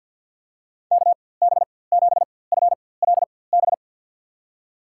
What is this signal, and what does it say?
Morse code, heard as a keyed tone, about 53 wpm, spaced as for 24 wpm, 700 Hz. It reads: KB7FLB
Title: USA Call Signs - 6 Characters - 4x Character Spacing - 50wpm